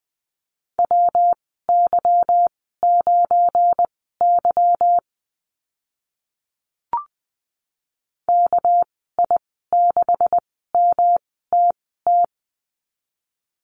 Morse code, heard as a keyed tone, about 20 wpm, spaced as for 15 wpm, 700 Hz. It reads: WY9Y KI6MTT
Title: 200 Lesson 32g - Letter k - Call Signs 20wpm